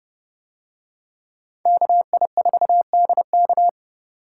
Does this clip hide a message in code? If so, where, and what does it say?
Yes: Morse code from 1.7 to 3.7 s, heard as a keyed tone, about 30 wpm, 700 Hz. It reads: KI4DK